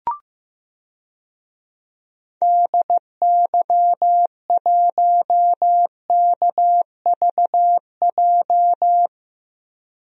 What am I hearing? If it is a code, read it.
Morse code, heard as a keyed tone, about 15 wpm, 700 Hz. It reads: DY1KVJ
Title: DXCC-100 Call Signs - 6 Characters - 15wpm